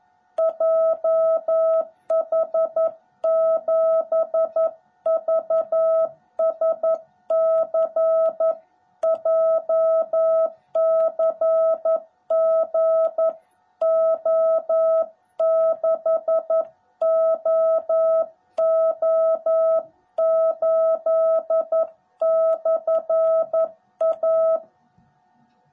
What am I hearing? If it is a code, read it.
Morse code, heard as a keyed tone, about 11 wpm, 660 Hz. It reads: JH7VSC JCGO6OO8/A
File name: jh8cbh / 2021cw / cw18b